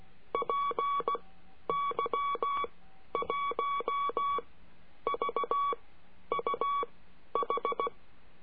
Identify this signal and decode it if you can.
Morse code, heard as a keyed tone, about 16 wpm, spaced as for 10 wpm, 1130 Hz. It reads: PY1VUH
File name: CURSO DE TELEGRAFIA